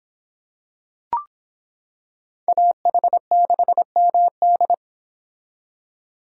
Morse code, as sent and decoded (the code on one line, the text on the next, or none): .- .... -.... -- -..
AH6MD